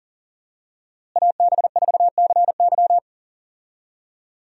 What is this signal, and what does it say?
Morse code, heard as a keyed tone, about 38 wpm, 700 Hz. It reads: AB4CY
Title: USA Call Signs - 5 Characters - Repeated 2x - 40wpm